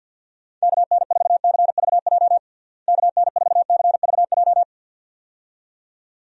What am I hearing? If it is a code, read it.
Morse code, heard as a keyed tone, about 50 wpm, 700 Hz. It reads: KN4CVJ KN4CVJ
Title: USA Call Signs - 6 Characters - Repeated 2x - 50wpm